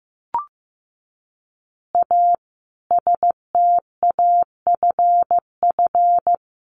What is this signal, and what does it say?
Morse code, heard as a keyed tone, about 15 wpm, 700 Hz. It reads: A STAFF
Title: Additional Easy Phrases 15wpm